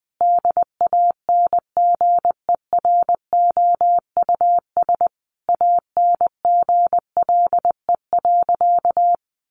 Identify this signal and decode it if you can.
Morse code, heard as a keyed tone, about 20 wpm, 700 Hz. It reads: DANGEROUS ANGLE.
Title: Sentences from Top 1500 Words 20wpm